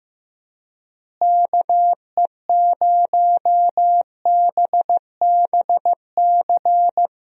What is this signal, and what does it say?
Morse code, heard as a keyed tone, about 15 wpm, 700 Hz. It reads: KE0BBC
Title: USA Call Signs - 6 Characters - Repeated 2x - 15wpm